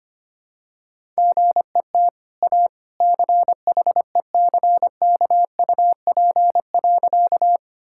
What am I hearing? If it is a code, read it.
Morse code, heard as a keyed tone, about 25 wpm, 700 Hz. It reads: GET A CHECKUP.
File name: Easy Sentences with Hints - 25wpm